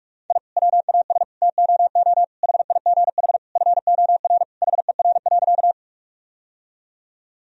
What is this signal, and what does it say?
Morse code, heard as a keyed tone, about 43 wpm, 700 Hz. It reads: I WAS TOO HIGH FOR HER.
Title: Sentences from Top 1000 Words - Review Part 1 - 45wpm